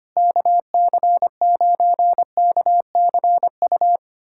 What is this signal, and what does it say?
Morse code, heard as a keyed tone, about 25 wpm, 700 Hz. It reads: KC9KCU
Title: USA Callsign DE Callsign 25wpm